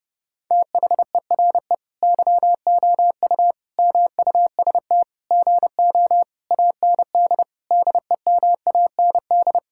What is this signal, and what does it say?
Morse code, heard as a keyed tone, about 32 wpm, 700 Hz. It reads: THERE YOU MUST GO AND DEMAND
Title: Sentences from Top 1600 Words 30wpm